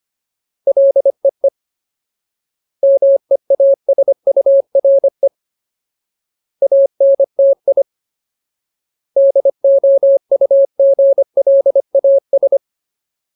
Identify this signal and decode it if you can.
Morse code, heard as a keyed tone, about 25 wpm, 550 Hz. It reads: LEE MEASURE ANTI DOUGLAS
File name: Ultimate Word List / 25WPM Ultimate